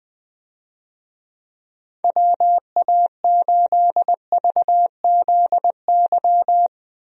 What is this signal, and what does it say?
Morse code, heard as a keyed tone, about 20 wpm, 700 Hz. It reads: WA8VZY
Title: USA Call Signs - 6 Characters - Repeated 3x - 20wpm